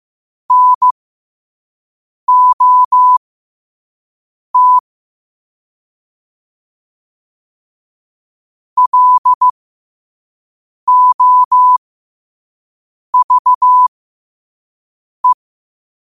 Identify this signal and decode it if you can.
Morse code, heard as a keyed tone, about 15 wpm, spaced as for 5 wpm, 1000 Hz. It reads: NOT LOVE
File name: qotdfiles / 5wpm